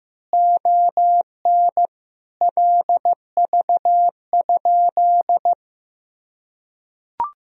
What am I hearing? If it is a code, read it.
Morse code, heard as a keyed tone, about 15 wpm, 700 Hz. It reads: ON LV?
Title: Sentences from Top 400 Words and Abbreviations 15wpm